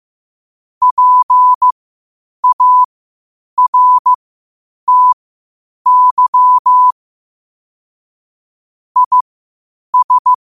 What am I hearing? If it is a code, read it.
Morse code, heard as a keyed tone, about 15 wpm, spaced as for 8 wpm, 1000 Hz. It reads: PARTY IS
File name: qotdfiles / 8wpm